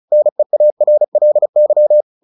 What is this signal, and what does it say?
Morse code, heard as a keyed tone, about 34 wpm, 600 Hz. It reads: NEARLY
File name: the old man and the sea 2 35wpm